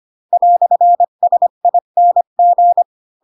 Morse code, heard as a keyed tone, about 25 wpm, 700 Hz. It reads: "SING
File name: Pinocchio / Pinocchio ch4 20 25 ik1hge